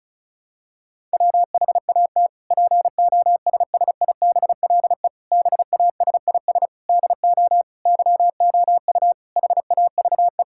Morse code, heard as a keyed tone, about 36 wpm, 700 Hz. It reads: WHAT POSSIBLE BASIS DO YOU HAVE
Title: Sentences from Top 1000 Words - Review Part 5 - 35wpm